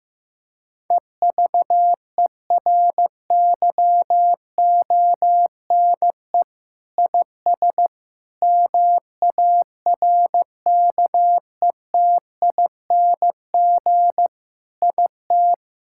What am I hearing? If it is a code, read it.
Morse code, heard as a keyed tone, about 15 wpm, 700 Hz. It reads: EVERYONE IS MARKETING IT
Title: Sentences from Top 1000 Words - Review Part 5 - 15wpm